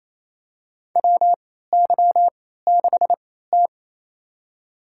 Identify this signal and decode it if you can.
Morse code, heard as a keyed tone, about 28 wpm, spaced as for 16 wpm, 700 Hz. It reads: WY6T
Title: USA Call Signs - 4 Characters - 3x Character Spacing - 28wpm